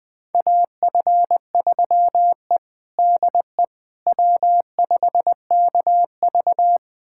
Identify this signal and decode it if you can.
Morse code, heard as a keyed tone, about 20 wpm, 700 Hz. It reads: AF3E DE W5KV
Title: USA Callsign DE Callsign - 4 Characters - 20wpm